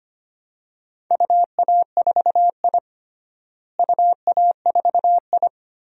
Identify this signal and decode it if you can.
Morse code, heard as a keyed tone, about 25 wpm, 700 Hz. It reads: UA4I UA4I
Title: DXCC-100 Call Signs - 4 Characters - Repeated 3x - 25wpm